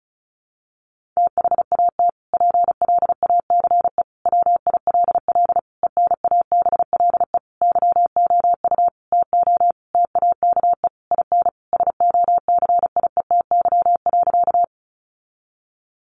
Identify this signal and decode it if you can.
Morse code, heard as a keyed tone, about 34 wpm, 700 Hz. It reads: THAT PLACE WILL ENABLE YOU TO TAKE IN SOCIETY.
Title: Sentences from Top 1500 Words 35wpm